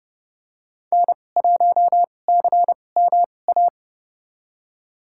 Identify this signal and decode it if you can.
Morse code, heard as a keyed tone, about 30 wpm, spaced as for 22 wpm, 700 Hz. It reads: N1CMA
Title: USA Call Signs - 5 Characters - 2x Character Spacing - 30wpm